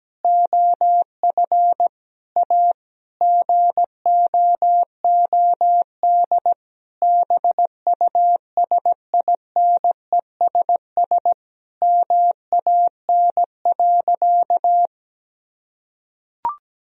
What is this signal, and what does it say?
Morse code, heard as a keyed tone, about 17 wpm, 700 Hz. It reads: OF A GOOD BUSINESS MAN.
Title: Sentences from Top 1000 Words - Review Part 4 - 17wpm